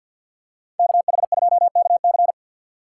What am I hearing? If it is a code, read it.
Morse code, heard as a keyed tone, about 53 wpm, 700 Hz. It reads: KH2KC
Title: USA Call Signs - 5 Characters - 4x Character Spacing - 50wpm